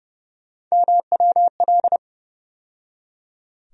Morse code, heard as a keyed tone, about 30 wpm, 700 Hz. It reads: MWL
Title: USA Call Sign Postfixes - 3 Letters 30wpm